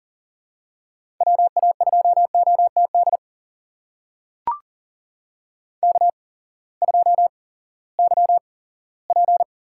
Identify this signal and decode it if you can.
Morse code, heard as a keyed tone, about 40 wpm, spaced as for 20 wpm, 700 Hz. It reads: WA2OTD K2YP
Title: USA Call Signs - 8x Character Spacing - 40wpm